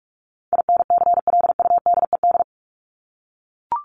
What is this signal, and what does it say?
Morse code, heard as a keyed tone, about 44 wpm, 700 Hz. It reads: INCLUDED
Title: ED Suffix - Top 100 Words with Base Words 45wpm